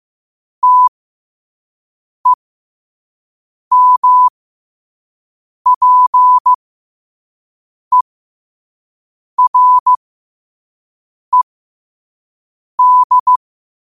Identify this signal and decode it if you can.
Morse code, heard as a keyed tone, about 15 wpm, spaced as for 5 wpm, 1000 Hz. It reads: TEMPERED
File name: qotdfiles / 5wpm